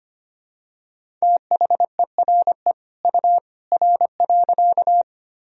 Morse code, heard as a keyed tone, about 25 wpm, 700 Hz. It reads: THERE U R.
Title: Sentences from Top 100 Words and Abbreviations 25wpm